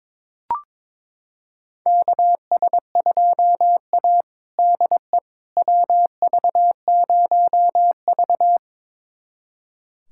Morse code, heard as a keyed tone, about 22 wpm, 700 Hz. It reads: KS2A DE WV0V
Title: USA Callsign DE Callsign - 4 Characters - 22wpm